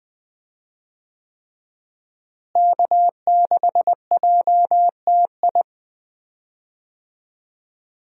Morse code, heard as a keyed tone, about 20 wpm, 700 Hz. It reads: K6JTI
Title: USA Call Signs - 5 Characters - Repeated 3x - 20wpm